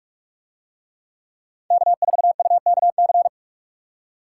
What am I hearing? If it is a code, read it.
Morse code, heard as a keyed tone, about 43 wpm, 700 Hz. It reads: K4UKC